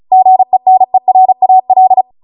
Morse code, heard as a keyed tone, about 36 wpm, 750 Hz. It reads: GENERAL
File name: Morse / 35WPM